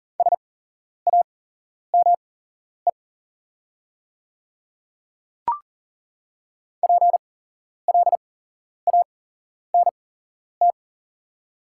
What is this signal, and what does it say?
Morse code, heard as a keyed tone, about 40 wpm, spaced as for 11 wpm, 690 Hz. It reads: SAME PLANT
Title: 8x Character Spacing - Top 500 Words - 40wpm